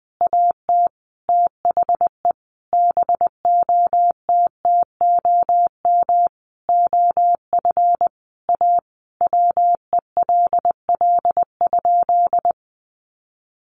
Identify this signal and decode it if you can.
Morse code, heard as a keyed tone, about 20 wpm, 700 Hz. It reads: AT THE BOTTOM OF A WELL?
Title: Sentences from Top 900 Words 20wpm